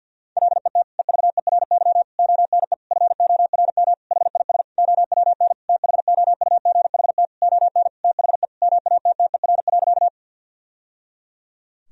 Morse code, heard as a keyed tone, about 52 wpm, 700 Hz. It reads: LET EVERY ONE FORM HIS OWN THOUGHT ON THE MATTER.
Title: Sentences from Top 1000 Words - Review Part 1 - 50wpm